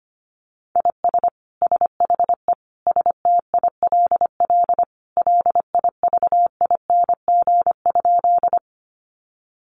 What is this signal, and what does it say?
Morse code, heard as a keyed tone, about 25 wpm, 700 Hz. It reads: IS SHE STILL LIVING?
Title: Sentences from Top 1100 Words 25wpm